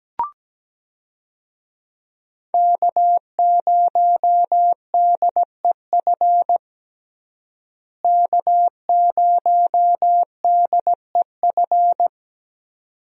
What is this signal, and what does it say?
Morse code, heard as a keyed tone, about 17 wpm, 700 Hz. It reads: K0DEF K0DEF
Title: USA Call Signs - 5 Characters - Repeated 2x - 17wpm